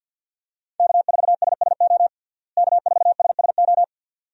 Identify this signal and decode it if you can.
Morse code, heard as a keyed tone, about 53 wpm, 700 Hz. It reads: K4SSO K4SSO
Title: USA Call Signs - 5 Characters - Repeated 2x - 50wpm